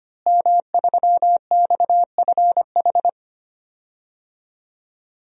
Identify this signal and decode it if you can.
Morse code, heard as a keyed tone, about 25 wpm, 700 Hz. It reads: M3XFH
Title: DXCC-100 Call Signs - 5 Characters - Repeated 2x - 25wpm